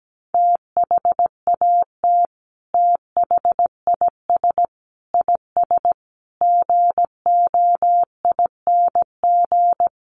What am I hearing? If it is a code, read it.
Morse code, heard as a keyed tone, about 17 wpm, 700 Hz. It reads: THAT THIS IS GOING
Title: Sets of 5 Words Encore2 17wpm